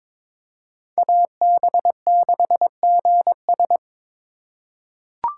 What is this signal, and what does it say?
Morse code, heard as a keyed tone, about 22 wpm, 690 Hz. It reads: AB6GS
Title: USA Call Signs - 5 Characters - 2x Character Spacing - 22wpm